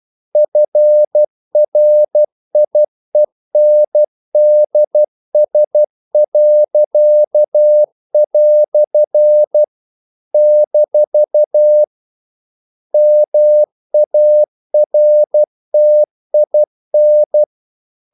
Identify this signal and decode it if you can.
Morse code, heard as a keyed tone, about 12 wpm, 600 Hz. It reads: FRIENDS." - MARTIN
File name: CW / Practice / EnglishQuotations0112